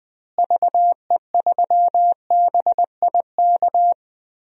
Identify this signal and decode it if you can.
Morse code, heard as a keyed tone, about 20 wpm, 700 Hz. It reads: VE3BIK
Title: DXCC-100 Call Signs - 6 Characters - Repeated 3x - 20wpm